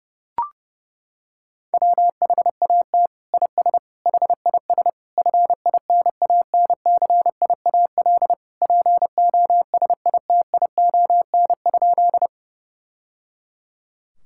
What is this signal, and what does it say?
Morse code, heard as a keyed tone, about 30 wpm, 700 Hz. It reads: WHAT IS HIS FINANCIAL POSITION?
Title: Sentences from Top 1000 Words - Review Part 5 - 30wpm